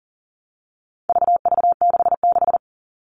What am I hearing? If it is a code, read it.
Morse code, heard as a keyed tone, about 40 wpm, 700 Hz. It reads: VV66
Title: All Cut Numbers - 4-Digit 40wpm